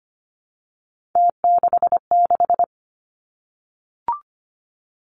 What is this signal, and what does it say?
Morse code, heard as a keyed tone, about 25 wpm, 700 Hz. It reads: T66
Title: Three Challenging Letter-Number Combinations 25wpm